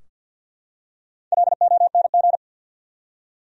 Morse code, heard as a keyed tone, about 50 wpm, 700 Hz. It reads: LONG
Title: Top 100 Words 50wpm